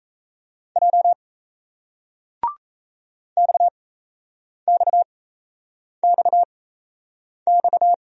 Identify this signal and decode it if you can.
Morse code, heard as a keyed tone, about 37 wpm, spaced as for 8 wpm, 700 Hz. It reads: J XXXX